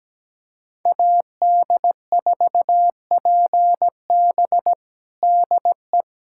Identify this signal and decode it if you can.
Morse code, heard as a keyed tone, about 17 wpm, 700 Hz. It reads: AD4PB DE
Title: USA Callsign DE Callsign - 5 Characters - 17wpm